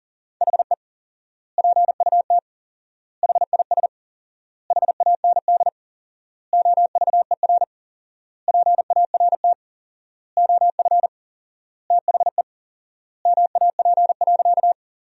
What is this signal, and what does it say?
Morse code, heard as a keyed tone, about 40 wpm, 700 Hz. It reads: HE PUT HIS HAND OVER PART OF THE MAP.
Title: 4x Word-Spacing - Sentences from Top 400 Words - 40wpm